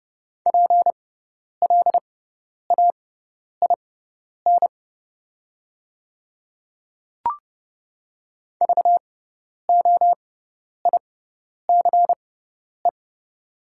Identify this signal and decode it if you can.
Morse code, heard as a keyed tone, about 30 wpm, spaced as for 10 wpm, 700 Hz. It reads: PLAIN VOICE
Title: 6x Character Spacing - Top 500 Words - 30wpm